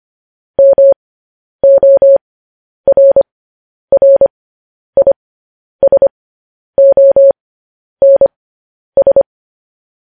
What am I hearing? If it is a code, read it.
Morse code, heard as a keyed tone, about 25 wpm, spaced as for 10 wpm, 550 Hz. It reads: MORRISONS
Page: news-headlines-in-morse-code-at-10-wpm